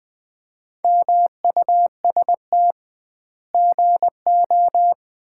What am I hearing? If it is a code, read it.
Morse code, heard as a keyed tone, about 20 wpm, 700 Hz. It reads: MUST GO
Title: Common QSO Phrases - 2x Word-Spacing - 20wpm